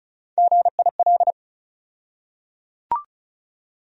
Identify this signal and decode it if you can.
Morse code, heard as a keyed tone, about 36 wpm, 700 Hz. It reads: GIL
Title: USA Call Sign Postfixes - 3 Letters 35wpm